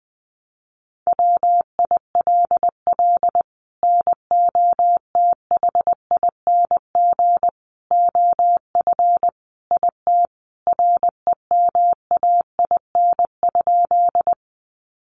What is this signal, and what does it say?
Morse code, heard as a keyed tone, about 20 wpm, 700 Hz. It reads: WILL NOTHING OF IT REMAIN?
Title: Sentences from Top 1200 Words 20wpm